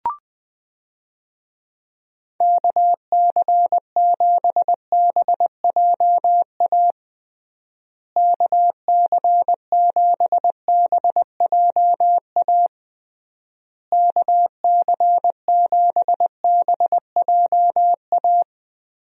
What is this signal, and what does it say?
Morse code, heard as a keyed tone, about 20 wpm, 700 Hz. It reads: KC7BJA KC7BJA KC7BJA
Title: USA Call Signs - 6 Characters - Repeated 3x - 20wpm